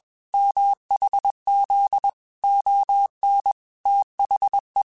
Morse code, heard as a keyed tone, about 21 wpm, 800 Hz. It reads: MHZ ON THE